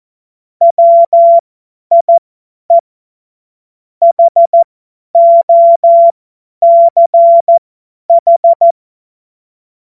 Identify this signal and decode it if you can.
Morse code, heard as a keyed tone, about 14 wpm, spaced as for 10 wpm, 680 Hz. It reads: WIE HOCH
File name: Morsekurs / Witze / Witz 28. Dezember 2021, 50 F70